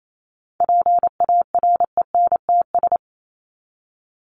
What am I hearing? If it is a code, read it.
Morse code, heard as a keyed tone, about 28 wpm, 700 Hz. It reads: PARENTS
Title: ES:S Suffix - Top 100 Words with Base Words 28wpm